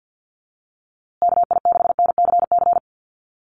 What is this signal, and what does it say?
Morse code, heard as a keyed tone, about 50 wpm, 700 Hz. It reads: KI6NCC
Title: USA Call Signs Repeated 3x 50wpm